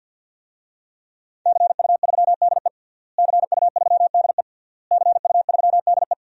Morse code, heard as a keyed tone, about 52 wpm, 700 Hz. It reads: CU3DE CU3DE CU3DE
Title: DXCC-100 Call Signs - 5 Characters - Repeated 3x - 50wpm